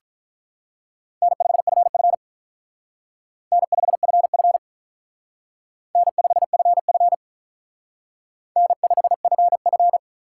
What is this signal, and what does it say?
Morse code, heard as a keyed tone, about 43 wpm, 700 Hz. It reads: N5FF N5FF N5FF N5FF
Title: Speed Racing - USA 1x2 Call Signs 35wpm